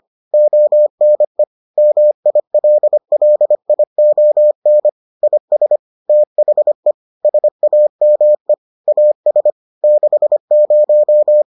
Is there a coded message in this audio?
Yes: Morse code from 0.3 to 11.4 s, heard as a keyed tone, about 25 wpm, 600 Hz. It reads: ONE MILLION IS THE SAME AS 60